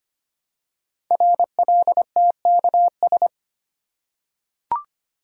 Morse code, heard as a keyed tone, about 25 wpm, 700 Hz. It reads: RLTKS
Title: Code Groups - 5 Letters - 25wpm